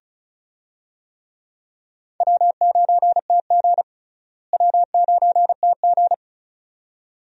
Morse code, heard as a keyed tone, about 34 wpm, 700 Hz. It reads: W9TG W9TG
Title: USA Call Signs - 4 Characters - Repeated 2x - 35wpm